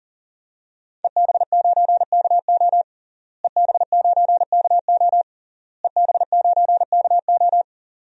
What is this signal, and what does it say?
Morse code, heard as a keyed tone, about 40 wpm, 700 Hz. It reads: EB9KO EB9KO EB9KO
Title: DXCC-100 Call Signs - 5 Characters - Repeated 3x - 40wpm